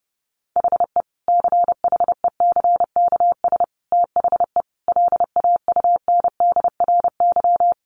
Morse code, heard as a keyed tone, about 30 wpm, 700 Hz. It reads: HE CHECKS THE LAUNDRY
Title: Easy Phrases 30wpm